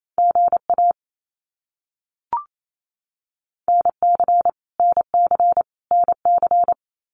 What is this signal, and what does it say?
Morse code, heard as a keyed tone, about 28 wpm, 700 Hz. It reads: GA NC NC NC
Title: US States Repeated 3x 28wpm